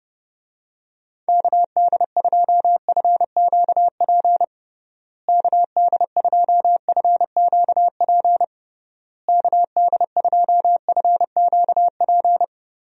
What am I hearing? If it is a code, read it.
Morse code, heard as a keyed tone, about 31 wpm, 700 Hz. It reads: KD2FQP KD2FQP KD2FQP
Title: USA Call Signs - 6 Characters - Repeated 3x - 30wpm